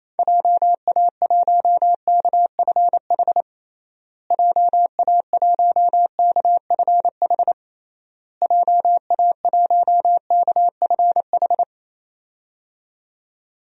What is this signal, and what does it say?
Morse code, heard as a keyed tone, about 27 wpm, 700 Hz. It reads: JA1KFH JA1KFH JA1KFH
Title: DXCC-100 Call Signs - 6 Characters - Repeated 3x - 28wpm